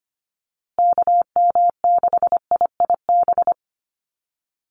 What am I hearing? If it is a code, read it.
Morse code, heard as a keyed tone, about 25 wpm, 700 Hz. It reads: KM6IIB
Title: USA Call Signs Repeated 2x 25wpm